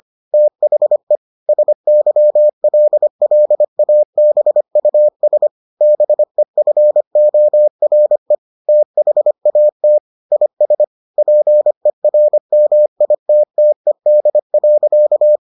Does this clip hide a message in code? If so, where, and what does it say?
Yes: Morse code from 0.3 to 15.4 s, heard as a keyed tone, about 25 wpm, 600 Hz. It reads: THE SYLLABUS BEFORE THAT IS PERMITTED.